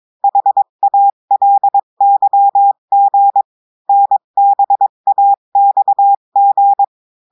Morse code, heard as a keyed tone, about 22 wpm, 800 Hz. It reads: HALYG NBAXG
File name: cours-du-mercredi / audios / cours-853-du-2-mai-2024-etxte chiffre et texte clair-a-20-wpm